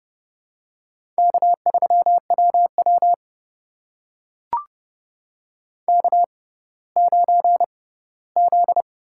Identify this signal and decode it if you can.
Morse code, heard as a keyed tone, about 30 wpm, spaced as for 17 wpm, 700 Hz. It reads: K3WW K9Z
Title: USA Call Signs - 4 Characters - 6x Character Spacing - 30wpm